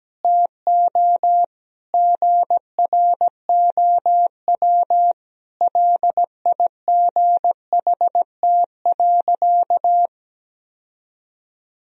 Morse code, heard as a keyed tone, about 17 wpm, 700 Hz. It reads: TO GROW LIGHT.
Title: Sentences from Top 1000 Words - Review Part 1 - 17wpm